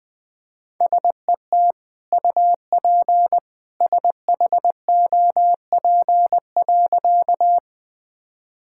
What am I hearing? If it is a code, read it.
Morse code, heard as a keyed tone, about 20 wpm, 700 Hz. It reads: SET UP SHOP.
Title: Sentences from Top 1600 Words 20wpm